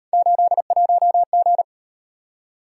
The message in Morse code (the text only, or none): 81G